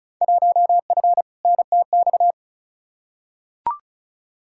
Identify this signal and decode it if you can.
Morse code, heard as a keyed tone, about 34 wpm, 700 Hz. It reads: 1F NTX
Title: Field Day Exchanges Encore - 35wpm